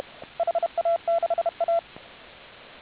Morse code, heard as a keyed tone, about 32 wpm, 680 Hz, with radio noise. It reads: HA6A